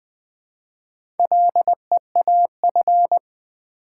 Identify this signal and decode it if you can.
Morse code, heard as a keyed tone, about 20 wpm, 700 Hz. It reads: LEAF
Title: Build A Word - 1 Letter At A Time - 4-Letter Words - 20wpm